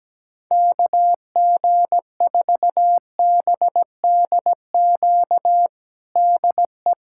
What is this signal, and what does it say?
Morse code, heard as a keyed tone, about 17 wpm, 700 Hz. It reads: KG4BDQ DE
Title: USA Callsign DE Callsign 17wpm